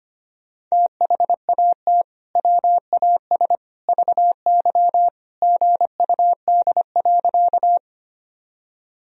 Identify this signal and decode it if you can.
Morse code, heard as a keyed tone, about 25 wpm, 700 Hz. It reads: THAT WAS VY GUD.
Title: Sentences from Top 200 Words and Abbreviations 25wpm